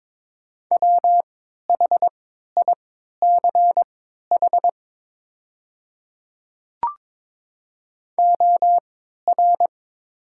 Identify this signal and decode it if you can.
Morse code, heard as a keyed tone, about 22 wpm, spaced as for 12 wpm, 690 Hz. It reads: WHICH OR
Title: 3x Character Spacing - Top 200 Words - 22wpm